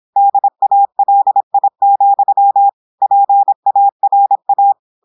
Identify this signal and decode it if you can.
Morse code, heard as a keyed tone, about 26 wpm, 800 Hz. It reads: DALI, PARA
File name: portugais texte en clair 2 vitesse 26 mots minutes